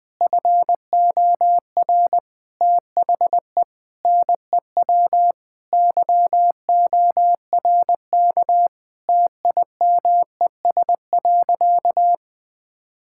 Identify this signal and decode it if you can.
Morse code, heard as a keyed tone, about 21 wpm, 700 Hz. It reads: FOR THE NEW YORK TIMES.
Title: Sentences with 5-Word Phrases - 20wpm